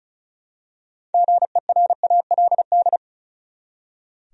Morse code, heard as a keyed tone, about 36 wpm, 700 Hz. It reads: GERALD
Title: USA Common Names - 35wpm